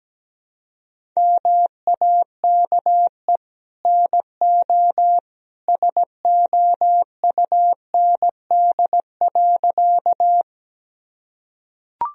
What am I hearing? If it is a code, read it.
Morse code, heard as a keyed tone, about 17 wpm, 700 Hz. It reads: MAKE NO SOUND.
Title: Sentences from Top 1000 Words - Review Part 1 - 17wpm